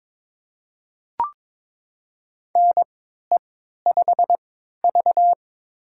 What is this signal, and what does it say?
Morse code, heard as a keyed tone, about 22 wpm, spaced as for 13 wpm, 700 Hz. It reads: NE5V
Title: USA Call Signs - 3x Character Spacing - 22wpm